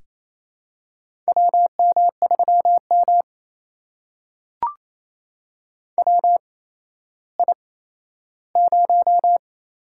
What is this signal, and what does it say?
Morse code, heard as a keyed tone, about 28 wpm, 700 Hz. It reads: W M 3 M W I 0
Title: USA Call Signs - 4 Characters - 8x Character Spacing - 28wpm